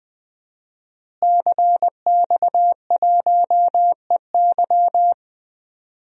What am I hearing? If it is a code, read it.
Morse code, heard as a keyed tone, about 21 wpm, 700 Hz. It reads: CX1EY
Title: DXCC-100 Call Signs - 5 Characters - Repeated 3x - 20wpm